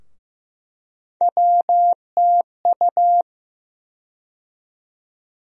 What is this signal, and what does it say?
Morse code, heard as a keyed tone, about 15 wpm, 700 Hz. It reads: WTU
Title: USA Call Sign Postfixes - 3 Letters 15wpm